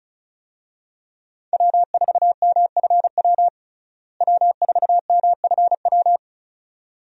Morse code, heard as a keyed tone, about 34 wpm, 700 Hz. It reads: W4MFW W4MFW
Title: USA Call Signs - 5 Characters - Repeated 2x - 35wpm